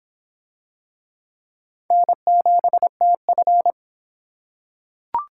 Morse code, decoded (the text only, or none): N7TF